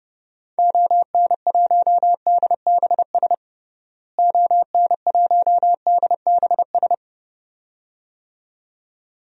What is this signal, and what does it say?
Morse code, heard as a keyed tone, about 32 wpm, 700 Hz. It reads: ON1DBS ON1DBS
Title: DXCC-100 Call Signs - 6 Characters - Repeated 2x - 30wpm